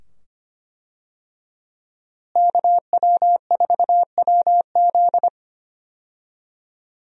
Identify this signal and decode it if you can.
Morse code, heard as a keyed tone, about 25 wpm, 700 Hz. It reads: KW4WZ
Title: USA Call Signs - 5 Characters - Repeated 2x - 25wpm